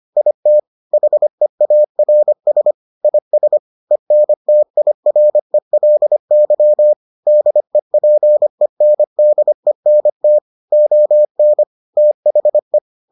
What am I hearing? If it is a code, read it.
Morse code, heard as a keyed tone, about 25 wpm, 600 Hz. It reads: IT HEARS IS ENTIRELY DEPENDENT ON THE